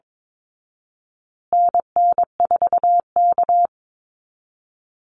Morse code, heard as a keyed tone, about 22 wpm, 700 Hz. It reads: NN4K
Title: Call Signs Repeated 2x - 22wpm